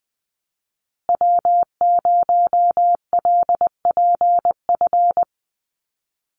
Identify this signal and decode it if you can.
Morse code, heard as a keyed tone, about 20 wpm, 700 Hz. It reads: W0LPF
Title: Call Signs Repeated 3x - 20wpm